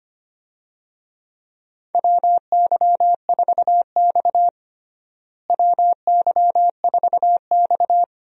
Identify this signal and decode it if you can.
Morse code, heard as a keyed tone, about 25 wpm, 700 Hz. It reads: WY4X WY4X
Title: USA Call Signs - 4 Characters - Repeated 2x - 25wpm